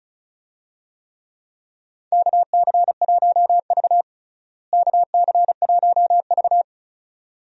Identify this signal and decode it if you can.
Morse code, heard as a keyed tone, about 36 wpm, 700 Hz. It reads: KC1V KC1V
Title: USA Call Signs - 4 Characters - Repeated 2x - 35wpm